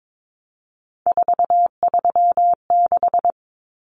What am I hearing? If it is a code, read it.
Morse code, heard as a keyed tone, about 22 wpm, 700 Hz. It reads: 436
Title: Numbers - 3-Digit 22wpm